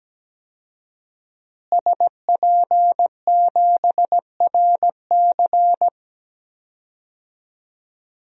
Morse code, heard as a keyed tone, about 17 wpm, 700 Hz. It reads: SP7RC
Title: DXCC-100 Call Signs - 5 Characters - Repeated 2x - 17wpm